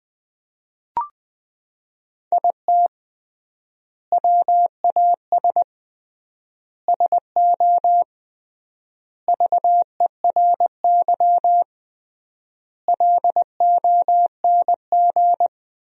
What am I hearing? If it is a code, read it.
Morse code, heard as a keyed tone, about 20 wpm, 700 Hz. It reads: IT WAS SO VERY LONG
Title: 3x Word-Spacing - Sentences from Top 500 Words - 20wpm